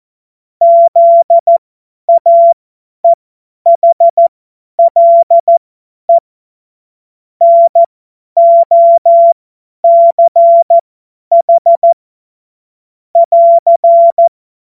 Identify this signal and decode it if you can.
Morse code, heard as a keyed tone, about 14 wpm, spaced as for 10 wpm, 680 Hz. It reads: ZAEHLE NOCH <AR>
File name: Morsekurs / Witze / Witz 06. September 2022 50 F70